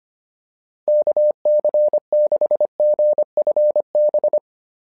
Morse code, heard as a keyed tone, about 25 wpm, 590 Hz. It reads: KC6GFB
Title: SOTA QSO Practice 25wpm